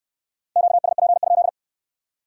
Morse code, heard as a keyed tone, about 61 wpm, 700 Hz. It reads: QSP?